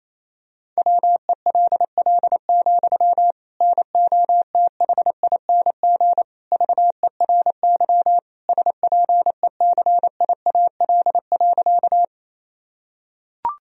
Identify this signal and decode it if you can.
Morse code, heard as a keyed tone, about 27 wpm, 700 Hz. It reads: WELL, NOTHING VERY SPECIAL.
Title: Sentences from Top 1000 Words - Review Part 3 - 28wpm